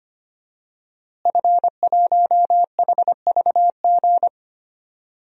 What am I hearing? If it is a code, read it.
Morse code, heard as a keyed tone, about 25 wpm, 700 Hz. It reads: F1HVG
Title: DXCC-100 Call Signs - 5 Characters - Repeated 2x - 25wpm